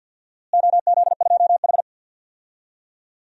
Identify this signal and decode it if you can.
Morse code, heard as a keyed tone, about 50 wpm, 700 Hz. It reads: OZ2H
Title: DXCC-100 Call Signs - 4 Characters - Repeated 3x - 50wpm